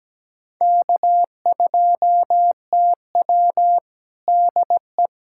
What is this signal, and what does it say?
Morse code, heard as a keyed tone, about 17 wpm, 700 Hz. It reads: K2TW DE
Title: Practice QSO - Exchange 4 - Ending - 17wpm